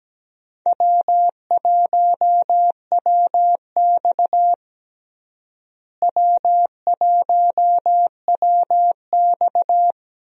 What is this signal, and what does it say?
Morse code, heard as a keyed tone, about 17 wpm, 700 Hz. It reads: W1WX W1WX
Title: USA Call Signs - 4 Characters - Repeated 3x - 17wpm